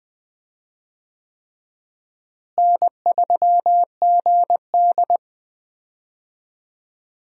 Morse code, heard as a keyed tone, about 20 wpm, 700 Hz. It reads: N3GD
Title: USA Call Signs - 4 Characters - Repeated 3x - 20wpm